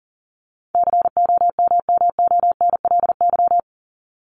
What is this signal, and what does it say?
Morse code, heard as a keyed tone, about 40 wpm, 700 Hz. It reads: COMMONLY
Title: LY Suffix - Top 100 Words with Base Words 40wpm